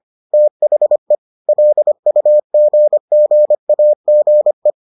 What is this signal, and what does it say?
Morse code, heard as a keyed tone, about 25 wpm, 600 Hz. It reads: THE LUGGAGE